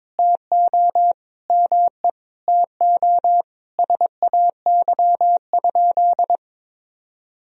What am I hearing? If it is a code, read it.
Morse code, heard as a keyed tone, about 22 wpm, 700 Hz. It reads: TO ME TO SAY?
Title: Sentences from Top 200 Words and Abbreviations 22wpm